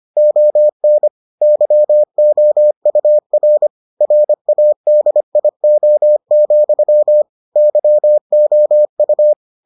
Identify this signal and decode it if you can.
Morse code, heard as a keyed tone, about 25 wpm, 600 Hz. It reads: ON YOUR RADIO, YOU